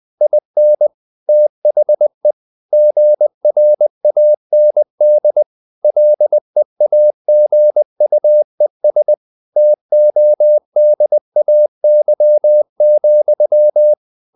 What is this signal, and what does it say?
Morse code, heard as a keyed tone, about 20 wpm, 600 Hz. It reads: IN THE GRAND LEAGUES TODAY,
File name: the old man and the sea / the old man and the sea part 1 20wpm